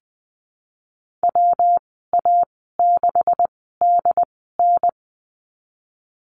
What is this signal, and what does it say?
Morse code, heard as a keyed tone, about 20 wpm, spaced as for 15 wpm, 700 Hz. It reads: WA6DN
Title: 167 Lesson 27g - Number 7 - Call Signs 20wpm